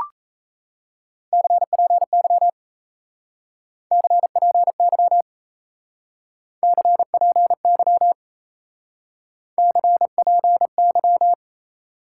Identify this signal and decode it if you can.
Morse code, heard as a keyed tone, about 36 wpm, 700 Hz. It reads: CPY CPY CPY CPY